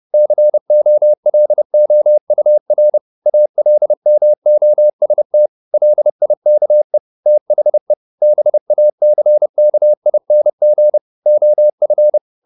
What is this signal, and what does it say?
Morse code, heard as a keyed tone, about 30 wpm, 600 Hz. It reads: COLOUR ALMOST LIKE THE BACKING OF
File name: the old man and the sea / the old man and the sea 1 30wpm